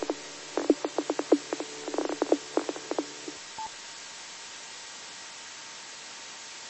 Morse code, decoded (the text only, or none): none